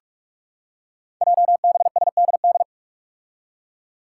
Morse code, heard as a keyed tone, about 46 wpm, 700 Hz. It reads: JBSDD
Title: Five Challenging Letter-Number Combinations 45wpm